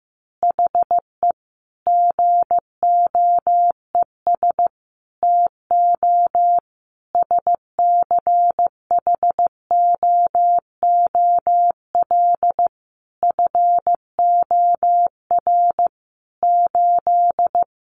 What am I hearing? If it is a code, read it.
Morse code, heard as a keyed tone, about 15 wpm, 700 Hz. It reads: HE GOES TO SCHOOL FOR 8
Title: Additional Easy Phrases 15wpm